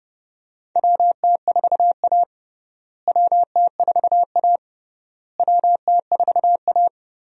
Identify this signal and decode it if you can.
Morse code, heard as a keyed tone, about 32 wpm, 700 Hz. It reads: WT4A WT4A WT4A
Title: USA Call Signs - 4 Characters - Repeated 3x - 30wpm